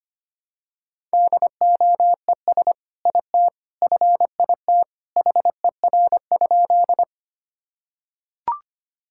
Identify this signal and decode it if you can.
Morse code, heard as a keyed tone, about 25 wpm, 700 Hz. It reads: DOES IT FIT HER?
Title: Sentences from Top 1000 Words - Review Part 5 - 25wpm